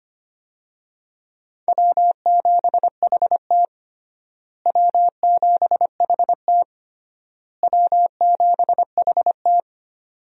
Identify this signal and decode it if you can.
Morse code, heard as a keyed tone, about 25 wpm, 700 Hz. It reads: W7HT W7HT W7HT
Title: USA Call Signs - 4 Characters - Repeated 3x - 25wpm